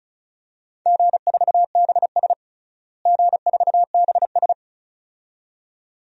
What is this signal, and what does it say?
Morse code, heard as a keyed tone, about 36 wpm, 700 Hz. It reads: G4BS G4BS
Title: DXCC-100 Call Signs - 4 Characters - Repeated 2x - 35wpm